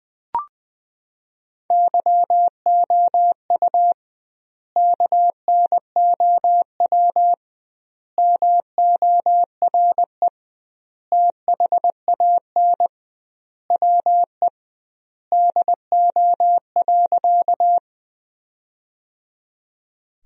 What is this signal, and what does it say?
Morse code, heard as a keyed tone, about 21 wpm, 700 Hz. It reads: YOU KNOW MORE THAN WE DO.
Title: 2x Word-Spacing - Sentences from Top 100 Words - 20wpm